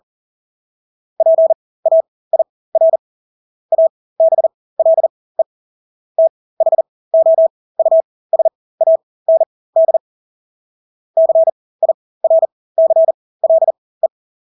Morse code, heard as a keyed tone, about 40 wpm, spaced as for 20 wpm, 650 Hz. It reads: PAIR ABLE THOUSAND CIRCLE
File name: Engelske ord 40-20 wpm